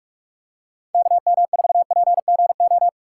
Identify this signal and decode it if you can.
Morse code, heard as a keyed tone, about 49 wpm, 700 Hz. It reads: KM4PGO